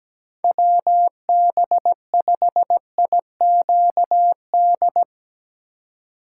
Morse code, heard as a keyed tone, about 17 wpm, 700 Hz. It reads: WB5IQD
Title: USA Call Signs - 6 Characters - 2x Character Spacing - 17wpm